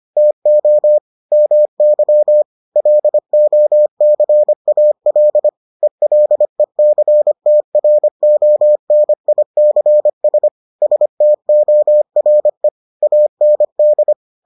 Morse code, heard as a keyed tone, about 25 wpm, 600 Hz. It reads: TO MY LOCAL ELECTRONICS STORE AND